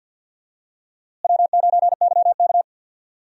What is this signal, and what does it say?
Morse code, heard as a keyed tone, about 52 wpm, 700 Hz. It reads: W9YK
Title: USA Call Signs - 4 Characters - Repeated 3x - 50wpm